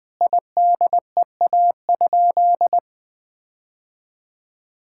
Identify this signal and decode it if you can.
Morse code, heard as a keyed tone, about 20 wpm, 700 Hz. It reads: IDEA?
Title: Sentences with 8-Word Phrases - 20wpm